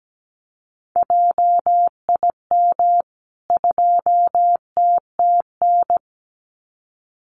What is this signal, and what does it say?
Morse code, heard as a keyed tone, about 17 wpm, 700 Hz. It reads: JIM 2TTN
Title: CWT Exchange 17wpm